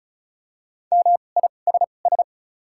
Morse code, heard as a keyed tone, about 36 wpm, 700 Hz. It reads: M I S S
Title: 2x Character Spacing - Top 500 Words - 35wpm